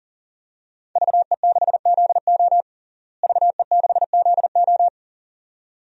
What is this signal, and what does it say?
Morse code, heard as a keyed tone, about 40 wpm, 700 Hz. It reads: VE6ZO VE6ZO
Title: DXCC-100 Call Signs - 5 Characters - Repeated 2x - 40wpm